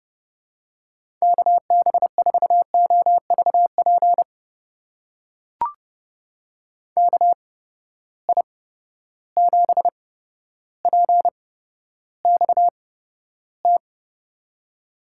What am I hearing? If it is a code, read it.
Morse code, heard as a keyed tone, about 30 wpm, spaced as for 13 wpm, 700 Hz. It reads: KB4OVP KI7PXT
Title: USA Call Signs - 6 Characters - 8x Character Spacing - 30wpm